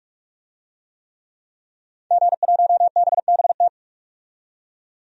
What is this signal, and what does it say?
Morse code, heard as a keyed tone, about 46 wpm, 700 Hz. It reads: G1BBT
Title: DXCC-100 Call Signs - 5 Characters - 45wpm